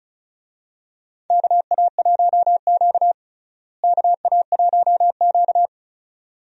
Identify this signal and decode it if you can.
Morse code, heard as a keyed tone, about 36 wpm, 700 Hz. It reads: KA1Q KA1Q
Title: USA Call Signs - 4 Characters - Repeated 2x - 35wpm